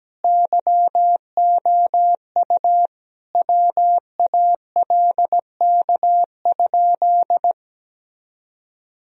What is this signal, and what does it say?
Morse code, heard as a keyed tone, about 17 wpm, 700 Hz. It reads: YOU WALK?
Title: Sentences from Top 1000 Words - Review Part 1 - 17wpm